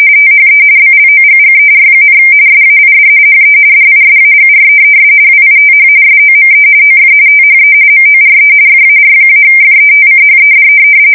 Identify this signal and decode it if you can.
RTTY, about 45.45 baud. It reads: THE BLOGGER, SAID HE DISAGREES WITH SOME OF THE MORE FAR-FETCHED
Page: 170 Shift 2125/2295